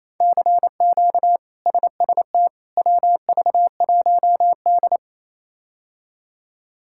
Morse code, heard as a keyed tone, about 27 wpm, 700 Hz. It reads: CQ SST WV1D
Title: K1USN SST QSO Practice - 28wpm